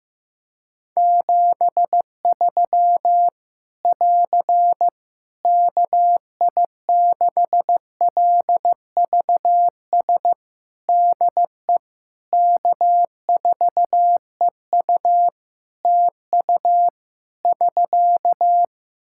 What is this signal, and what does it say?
Morse code, heard as a keyed tone, about 15 wpm, 700 Hz. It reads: 73 <AR> KI6LVS DE K4EU TU <SK>
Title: K1USN SST QSO Practice 15wpm